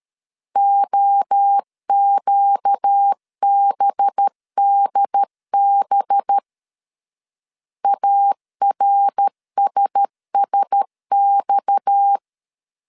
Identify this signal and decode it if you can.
Morse code, heard as a keyed tone, about 13 wpm, 800 Hz. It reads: OQBDB ARSSX
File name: CW / DET / 1 6 1500C S B 12wpm 3431